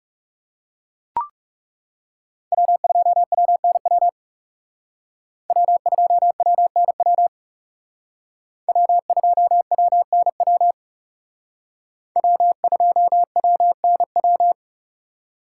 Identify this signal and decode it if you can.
Morse code, heard as a keyed tone, about 38 wpm, 700 Hz. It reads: W2WNW W2WNW W2WNW W2WNW